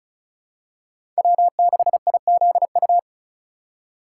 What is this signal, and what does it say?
Morse code, heard as a keyed tone, about 34 wpm, 700 Hz. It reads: W6IZU
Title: USA Call Signs - 5 Characters - 4x Character Spacing - 35wpm